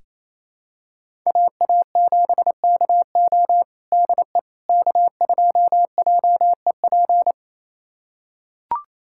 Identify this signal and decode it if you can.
Morse code, heard as a keyed tone, about 27 wpm, 700 Hz. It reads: AA7KO DE K2JEP
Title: USA Callsign DE Callsign - 5 Characters - 28wpm